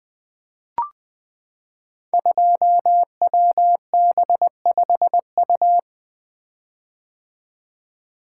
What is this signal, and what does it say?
Morse code, heard as a keyed tone, about 20 wpm, 700 Hz. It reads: 2WB5U
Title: Code Groups - 5 Letter:Numbers - 20wpm